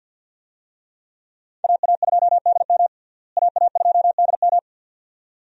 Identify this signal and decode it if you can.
Morse code, heard as a keyed tone, about 51 wpm, 700 Hz. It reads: AA2DM AA2DM
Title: USA Call Signs - 5 Characters - Repeated 2x - 50wpm